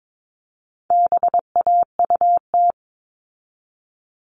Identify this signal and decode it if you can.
Morse code, heard as a keyed tone, about 22 wpm, 700 Hz. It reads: BAUT
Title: All Cut Numbers - 4-Digit 22wpm